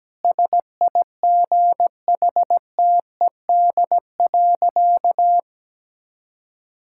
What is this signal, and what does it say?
Morse code, heard as a keyed tone, about 17 wpm, 700 Hz. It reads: SIGHTED.
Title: Pangram Challenge v2 - 17wpm